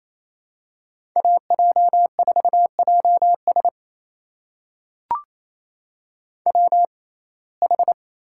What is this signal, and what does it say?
Morse code, heard as a keyed tone, about 28 wpm, spaced as for 20 wpm, 700 Hz. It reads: AJ4JS WH
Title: USA Call Signs - 5 Characters - 6x Character Spacing - 28wpm